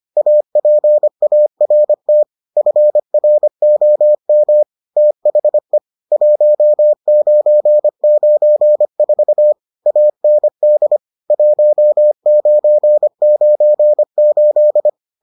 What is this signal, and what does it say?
Morse code, heard as a keyed tone, about 25 wpm, 600 Hz. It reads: APART FROM THE 1994 AND 1998